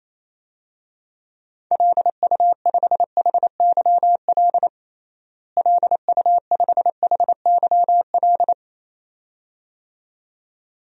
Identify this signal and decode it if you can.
Morse code, heard as a keyed tone, about 28 wpm, 700 Hz. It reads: LU5HYL LU5HYL
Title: DXCC-100 Call Signs - 6 Characters - Repeated 2x - 28wpm